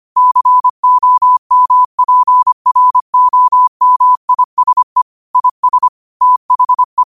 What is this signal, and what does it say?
Morse code, heard as a keyed tone, about 25 wpm, 1000 Hz. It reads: COMPROMISE IS THE
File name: qotdfiles / 25wpm